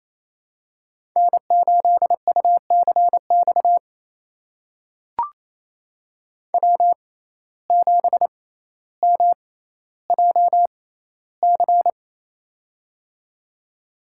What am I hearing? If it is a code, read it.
Morse code, heard as a keyed tone, about 28 wpm, spaced as for 14 wpm, 700 Hz. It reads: N8UCX W7MJC
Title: USA Call Signs - 5 Characters - 6x Character Spacing - 28wpm